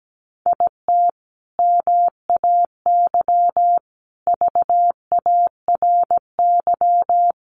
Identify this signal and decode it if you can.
Morse code, heard as a keyed tone, about 17 wpm, 700 Hz. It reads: IT MAY VARY
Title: Sentences from Top 1200 Words 17wpm